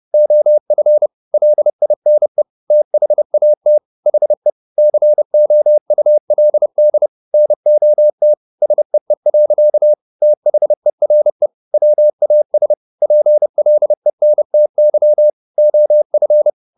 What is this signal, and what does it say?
Morse code, heard as a keyed tone, about 30 wpm, 600 Hz. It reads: OF LINE THAT HE COULD NOT SEE. THERE WAS PLENTY OF